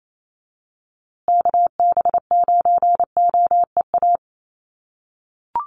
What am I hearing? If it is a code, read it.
Morse code, heard as a keyed tone, about 28 wpm, 700 Hz. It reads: KB9OEA
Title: USA Call Signs Repeated 2x 28wpm